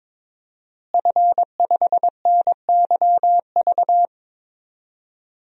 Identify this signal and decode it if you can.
Morse code, heard as a keyed tone, about 22 wpm, 700 Hz. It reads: F5NYV
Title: DXCC-100 Call Signs - 5 Characters - Repeated 3x - 22wpm